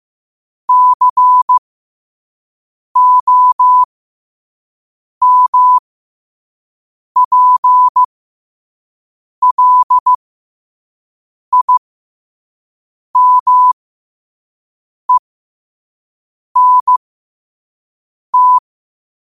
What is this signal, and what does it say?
Morse code, heard as a keyed tone, about 15 wpm, spaced as for 5 wpm, 1000 Hz. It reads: COMPLIMENT
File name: qotdfiles / 5wpm